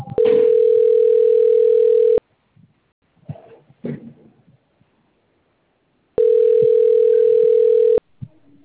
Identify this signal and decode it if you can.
Ringback tone (North America), 440 + 480 Hz.
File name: rg-1600-unknown- - - .40